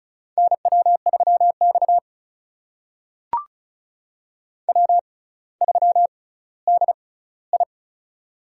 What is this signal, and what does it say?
Morse code, heard as a keyed tone, about 36 wpm, spaced as for 18 wpm, 700 Hz. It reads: NW3X W3DI
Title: USA Call Signs - 4 Characters - 6x Character Spacing - 35wpm